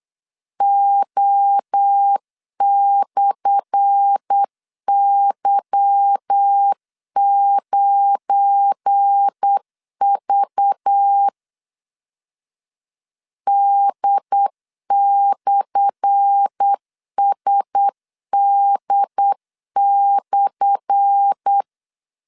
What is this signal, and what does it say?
Morse code, heard as a keyed tone, about 8 wpm, 800 Hz. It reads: O/Y9V D/SD/
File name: LEZIONI / 10 SIGNS / 10A SLASH 8wpm 3452